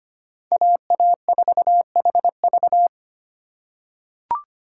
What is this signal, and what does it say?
Morse code, heard as a keyed tone, about 25 wpm, 700 Hz. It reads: AA4HV
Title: K1USN SST QSO Practice - 25wpm